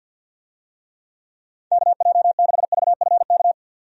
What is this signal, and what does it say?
Morse code, heard as a keyed tone, about 51 wpm, 700 Hz. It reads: KJ6VFK
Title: USA Call Signs - 6 Characters - 50wpm